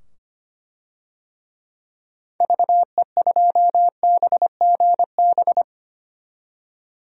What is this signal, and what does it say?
Morse code, heard as a keyed tone, about 25 wpm, 700 Hz. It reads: VE2BGB
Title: DXCC-100 Call Signs - 6 Characters - 25wpm